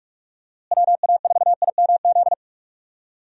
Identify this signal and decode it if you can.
Morse code, heard as a keyed tone, about 43 wpm, 700 Hz. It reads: WA4IMZ
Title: USA Call Signs - 6 Characters - Repeated 3x - 45wpm